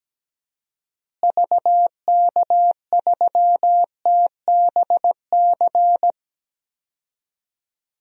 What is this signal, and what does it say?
Morse code, heard as a keyed tone, about 17 wpm, 700 Hz. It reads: VK3TBC
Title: DXCC-100 Call Signs - 6 Characters - Repeated 3x - 17wpm